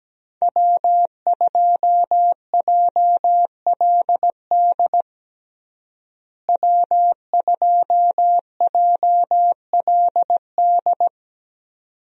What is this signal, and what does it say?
Morse code, heard as a keyed tone, about 17 wpm, 700 Hz. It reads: W2JLD W2JLD
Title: USA Call Signs - 5 Characters - Repeated 2x - 17wpm